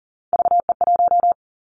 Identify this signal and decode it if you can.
Morse code, heard as a keyed tone, about 40 wpm, 700 Hz. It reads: VE1
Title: Callsign Prefixes for Top 100 DXCC - 40wpm